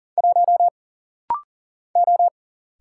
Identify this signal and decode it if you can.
Morse code, heard as a keyed tone, about 40 wpm, 700 Hz. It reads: 1 O
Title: Single Letter-Number Alt - Rapid-Fire 40wpm